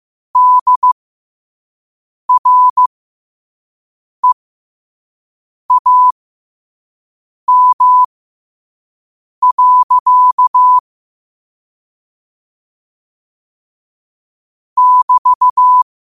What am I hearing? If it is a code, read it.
Morse code, heard as a keyed tone, about 15 wpm, spaced as for 5 wpm, 1000 Hz. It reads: DREAM. <BT>
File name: qotdfiles / 5wpm